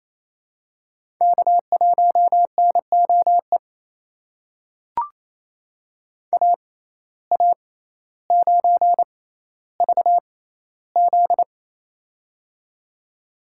Morse code, heard as a keyed tone, about 28 wpm, spaced as for 14 wpm, 700 Hz. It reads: K1NOE AA9VZ
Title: USA Call Signs - 6x Character Spacing - 28wpm